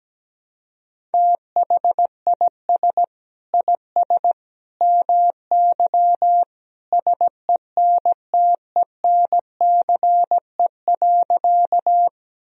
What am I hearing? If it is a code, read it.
Morse code, heard as a keyed tone, about 17 wpm, 700 Hz. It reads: THIS IS MY SENTENCE.
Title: Sentences from Top 1000 Words - Review Part 1 - 17wpm